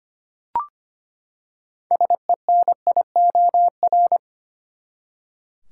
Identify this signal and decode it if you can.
Morse code, heard as a keyed tone, about 25 wpm, 700 Hz. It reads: SENIOR
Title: Six Letter Words - 25wpm